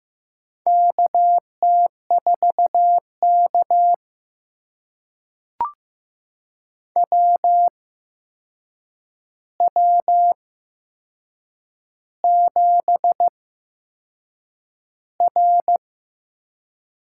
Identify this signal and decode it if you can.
Morse code, heard as a keyed tone, about 15 wpm, spaced as for 6 wpm, 700 Hz. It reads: KT4K WW7R
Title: USA Call Signs - 4 Characters - 8x Character Spacing - 15wpm